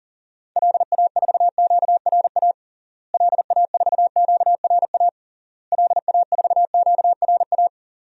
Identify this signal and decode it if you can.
Morse code, heard as a keyed tone, about 40 wpm, 700 Hz. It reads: LA4QRA LA4QRA LA4QRA
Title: DXCC-100 Call Signs - 6 Characters - Repeated 3x - 40wpm